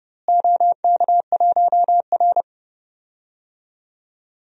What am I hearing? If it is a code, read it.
Morse code, heard as a keyed tone, about 31 wpm, 700 Hz. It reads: OK1R
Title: DXCC-100 Call Signs - 4 Characters - Repeated 2x - 30wpm